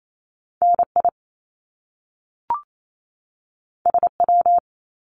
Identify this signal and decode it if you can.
Morse code, heard as a keyed tone, about 28 wpm, 700 Hz. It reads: NI SW
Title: Top Two Letter Combinations in Words 28wpm